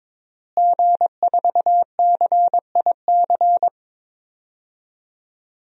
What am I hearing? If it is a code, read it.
Morse code, heard as a keyed tone, about 22 wpm, 700 Hz. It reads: G4CIC
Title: DXCC-100 Call Signs - 5 Characters - Repeated 3x - 22wpm